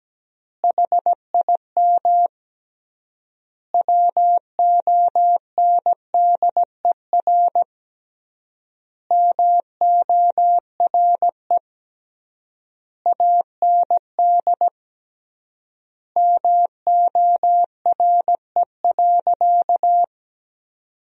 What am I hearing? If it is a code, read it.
Morse code, heard as a keyed tone, about 17 wpm, 700 Hz. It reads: HIM WONDER MORE AND MORE.
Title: 3x Word-Spacing - Sentences from Top 500 Words - 17wpm